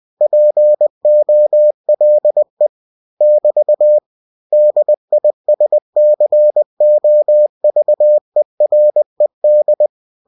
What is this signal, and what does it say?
Morse code, heard as a keyed tone, about 20 wpm, 600 Hz. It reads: POLE <BT> DISCOVERED